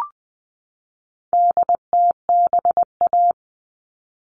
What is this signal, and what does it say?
Morse code, heard as a keyed tone, about 20 wpm, 700 Hz. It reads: DTBA
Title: All Cut Numbers - 4-Digit 20wpm